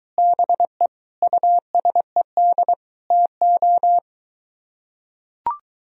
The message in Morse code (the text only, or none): BE USED TO